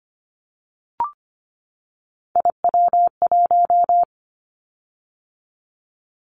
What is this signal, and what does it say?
Morse code, heard as a keyed tone, about 25 wpm, 700 Hz. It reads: IW1
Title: Callsign Prefixes for Top 50 DXCC - 25wpm